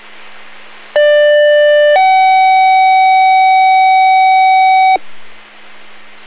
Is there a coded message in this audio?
Yes: two-tone page from 1.0 to 5.0 s, heard as a one-second tone, then a three-second tone, about 590 then 760 Hz, with noise.